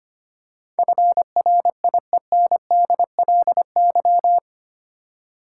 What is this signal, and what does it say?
Morse code, heard as a keyed tone, about 25 wpm, 700 Hz. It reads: FRIENDLY
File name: 8 Letter Words - First 3 Letters Given - 25wpm